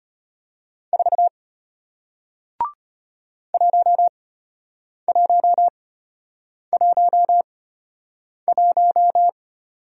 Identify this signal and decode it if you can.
Morse code, heard as a keyed tone, about 34 wpm, spaced as for 8 wpm, 700 Hz. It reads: 4 1111